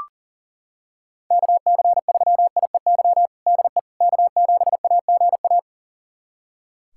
Morse code, heard as a keyed tone, about 40 wpm, 700 Hz. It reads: KC3IEY DE K7AGA
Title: USA Callsign DE Callsign 40wpm